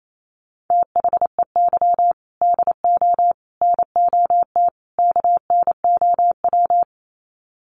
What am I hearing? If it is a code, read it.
Morse code, heard as a keyed tone, about 28 wpm, 700 Hz. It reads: THEY DO NOT KNOW
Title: Sets of 4 Words 28wpm